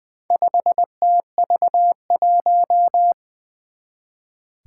Morse code, heard as a keyed tone, about 20 wpm, 700 Hz. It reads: 5TV1
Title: Four Challenging Letter-Number Combinations 20wpm